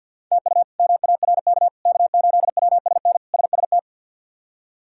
Morse code, heard as a keyed tone, about 45 wpm, 700 Hz. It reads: TU MARK K8WSN SST
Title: K1USN SST QSO Practice - 50wpm